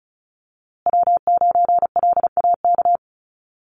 Morse code, heard as a keyed tone, about 34 wpm, 700 Hz. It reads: W9LAK
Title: USA Call Signs Repeated 3x 35wpm